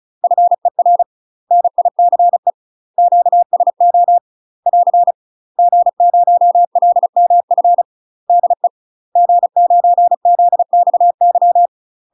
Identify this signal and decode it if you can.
Morse code, heard as a keyed tone, about 36 wpm, 700 Hz. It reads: FER NICE QSO <AR> G0LMF DE G9ZXY
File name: UK MASTER QSOs TRAINING / 35WPM / UK MASTER QSOs 0035